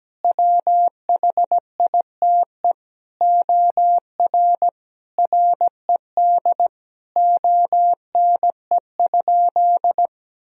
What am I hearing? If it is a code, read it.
Morse code, heard as a keyed tone, about 17 wpm, 700 Hz. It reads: WHITE OR RED ONE?
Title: Sentences from Top 1000 Words - Review Part 2 - 17wpm